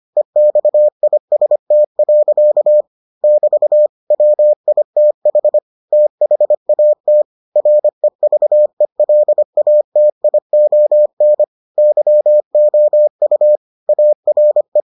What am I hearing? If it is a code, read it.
Morse code, heard as a keyed tone, about 25 wpm, 600 Hz. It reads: EXIST. <BT> WITH THAT REVELATION YOU ARE